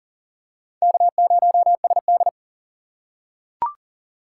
Morse code, heard as a keyed tone, about 40 wpm, 700 Hz. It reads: K0SD
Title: USA Call Signs - 4 Characters - 4x Character Spacing - 40wpm